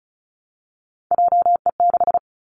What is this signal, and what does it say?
Morse code, heard as a keyed tone, about 36 wpm, 700 Hz. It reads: JE6
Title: Three Challenging Letter-Number Combinations 35wpm